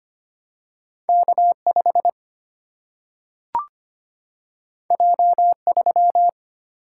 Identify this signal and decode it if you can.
Morse code, heard as a keyed tone, about 25 wpm, 700 Hz. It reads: K5 J3
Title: Single-Letter Single-Number 25wpm